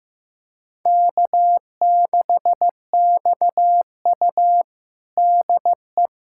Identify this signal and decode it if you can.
Morse code, heard as a keyed tone, about 16 wpm, 700 Hz. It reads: K6XU DE
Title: USA Callsign DE Callsign - 4 Characters - 15wpm